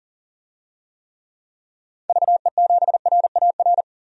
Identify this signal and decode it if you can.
Morse code, heard as a keyed tone, about 40 wpm, 700 Hz. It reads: VE7RAR
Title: DXCC-100 Call Signs - 6 Characters - 40wpm